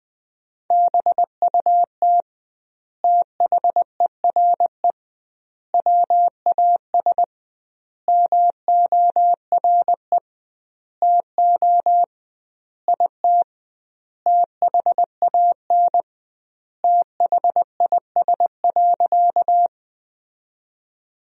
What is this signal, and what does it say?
Morse code, heard as a keyed tone, about 20 wpm, 700 Hz. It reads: BUT THERE WAS MORE TO IT THAN THIS.
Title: 2x Word-Spacing - Sentences from Top 100 Words - 20wpm